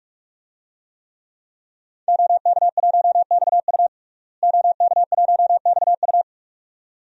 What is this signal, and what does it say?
Morse code, heard as a keyed tone, about 46 wpm, 700 Hz. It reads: OK1XU OK1XU
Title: DXCC-100 Call Signs - 5 Characters - Repeated 2x - 45wpm